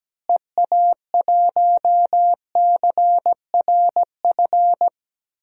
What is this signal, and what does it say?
Morse code, heard as a keyed tone, about 17 wpm, 700 Hz. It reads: EA1CRF
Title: DXCC-100 Call Signs - 6 Characters - Repeated 2x - 17wpm